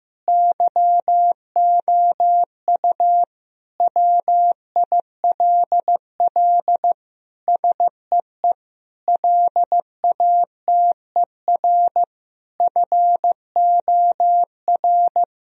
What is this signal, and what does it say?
Morse code, heard as a keyed tone, about 16 wpm, 700 Hz. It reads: YOU WILL SEE LATER FOR
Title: Sentences from Top 1000 Words - Review Part 4 - 15wpm